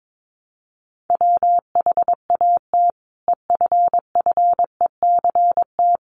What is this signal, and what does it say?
Morse code, heard as a keyed tone, about 22 wpm, 700 Hz. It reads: WHAT EFFECT
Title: Sentences from Top 1000 Words 22wpm